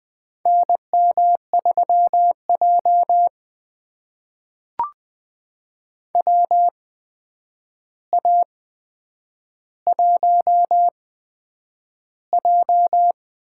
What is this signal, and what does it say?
Morse code, heard as a keyed tone, about 20 wpm, spaced as for 9 wpm, 700 Hz. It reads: NM3J WA1J
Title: USA Call Signs - 4 Characters - 8x Character Spacing - 20wpm